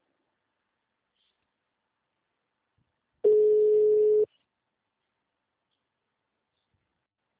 Ringback tone (Europe, 425 Hz).
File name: out- -2002- - - .253